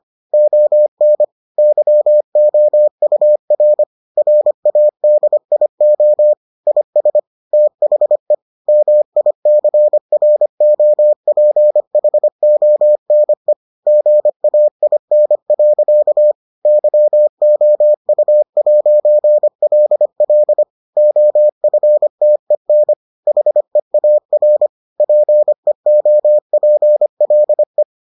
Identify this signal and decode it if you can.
Morse code, heard as a keyed tone, about 25 wpm, 600 Hz. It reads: ON YOUR RADIO IS THE MICROPHONE GAIN. YOU'LL OFTEN HEAR PEOPLE